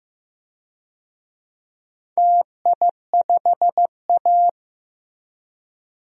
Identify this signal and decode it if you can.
Morse code, heard as a keyed tone, about 15 wpm, 700 Hz. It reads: TI5A
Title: DXCC-100 Call Signs - 4 Characters - Repeated 3x - 15wpm